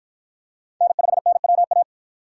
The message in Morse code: N5NWA

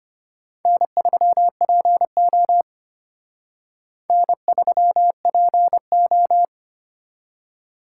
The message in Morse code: N3PO N3PO

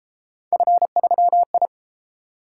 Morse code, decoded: F3I

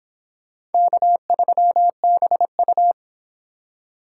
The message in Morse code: K3BU

